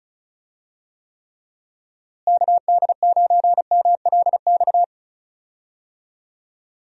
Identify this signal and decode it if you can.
Morse code, heard as a keyed tone, about 36 wpm, 700 Hz. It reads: KD9MLX
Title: USA Call Signs - 6 Characters - Repeated 3x - 35wpm